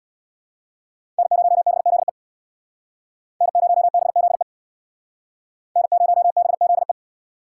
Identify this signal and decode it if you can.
Morse code, heard as a keyed tone, about 66 wpm, 700 Hz. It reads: N0BZE N0BZE N0BZE